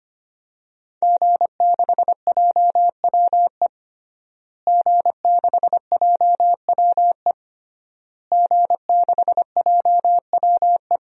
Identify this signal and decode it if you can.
Morse code, heard as a keyed tone, about 25 wpm, 700 Hz. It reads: G6JWE G6JWE G6JWE
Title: DXCC-100 Call Signs - 5 Characters - Repeated 3x - 25wpm